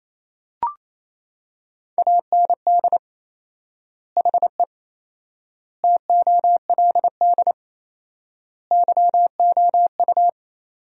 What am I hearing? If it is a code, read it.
Morse code, heard as a keyed tone, about 28 wpm, 700 Hz. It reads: AND HE TOLD YOU
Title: 4x Word-Spacing - Sentences from Top 500 Words - 28wpm